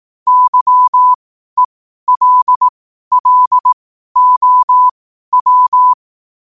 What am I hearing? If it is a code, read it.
Morse code, heard as a keyed tone, about 18 wpm, spaced as for 12 wpm, 1000 Hz. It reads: YELLOW
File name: qotdfiles / 12wpm